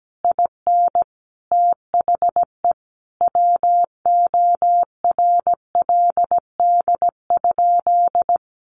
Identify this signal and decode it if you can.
Morse code, heard as a keyed tone, about 17 wpm, 700 Hz. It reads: IN THE WORLD?
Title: Sentences from Top 1400 Words 17wpm